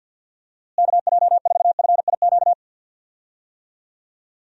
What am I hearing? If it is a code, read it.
Morse code, heard as a keyed tone, about 48 wpm, 700 Hz. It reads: KJ4FIQ
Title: USA Call Signs - 6 Characters - Repeated 3x - 50wpm